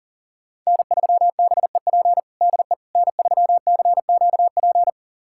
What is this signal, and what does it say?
Morse code, heard as a keyed tone, about 40 wpm, 700 Hz. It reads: N3BEP DE N3CQP
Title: USA Callsign DE Callsign - 5 Characters - 40wpm